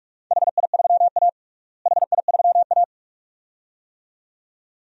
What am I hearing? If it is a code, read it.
Morse code, heard as a keyed tone, about 43 wpm, 700 Hz. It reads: HI3A HI3A
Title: DXCC-100 Call Signs - 4 Characters - Repeated 2x - 45wpm